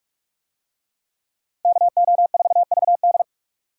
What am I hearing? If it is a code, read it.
Morse code, heard as a keyed tone, about 41 wpm, 700 Hz. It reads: KO4VD